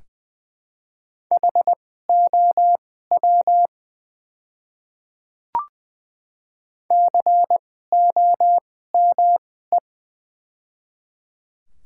Morse code, heard as a keyed tone, about 20 wpm, spaced as for 15 wpm, 700 Hz. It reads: HOW COME
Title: 2x Character Spacing - Top 100 Words - 20wpm